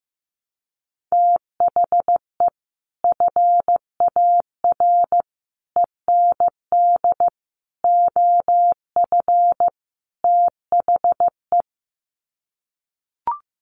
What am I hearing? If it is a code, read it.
Morse code, heard as a keyed tone, about 15 wpm, 700 Hz. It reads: THE FAR END OF THE
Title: Sets of 5 Words 15wpm